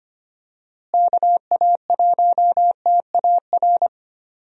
Morse code, heard as a keyed tone, about 25 wpm, 700 Hz. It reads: KA1TAR
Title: USA Call Signs - 6 Characters - Repeated 2x - 25wpm